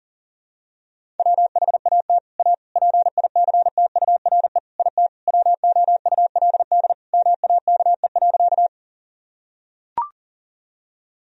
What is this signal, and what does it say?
Morse code, heard as a keyed tone, about 40 wpm, 700 Hz. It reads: WHAT A PICTURE IT WOULD MAKE.
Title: Sentences from Top 1000 Words - Review Part 1 - 40wpm